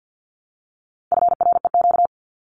Morse code, heard as a keyed tone, about 50 wpm, 700 Hz. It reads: FREQ
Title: Abbreviations Repeated 3x - 50wpm